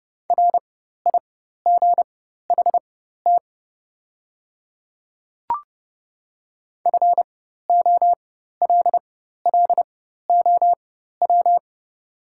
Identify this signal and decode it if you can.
Morse code, heard as a keyed tone, about 30 wpm, spaced as for 14 wpm, 700 Hz. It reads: RIGHT FOLLOW
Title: 4x Character Spacing - Top 200 Words - 30wpm